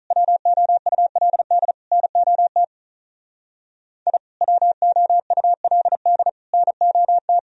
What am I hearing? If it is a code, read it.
Morse code, heard as a keyed tone, about 42 wpm, 700 Hz. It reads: WOULD NOT I WOULD NOT
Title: Speed-Racing - Sets of 3 Words 35wpm